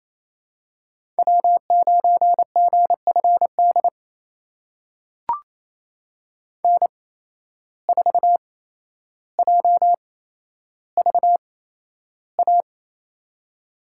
Morse code, heard as a keyed tone, about 28 wpm, spaced as for 12 wpm, 700 Hz. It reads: W9GFD N4JVA
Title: USA Call Signs - 5 Characters - 8x Character Spacing - 28wpm